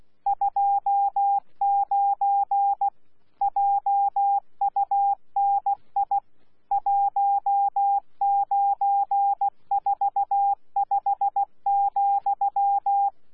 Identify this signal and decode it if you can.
Morse code, heard as a keyed tone, about 16 wpm, 800 Hz. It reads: 29 JUNI 1945,
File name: Z PI4VRZ / PI4VRZ 14-jun-2025